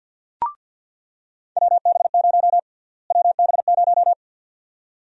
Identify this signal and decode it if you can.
Morse code, heard as a keyed tone, about 50 wpm, 700 Hz. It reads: WB0 WB0
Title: USA Call Sign Prefixes - Repeated 2x - 50wpm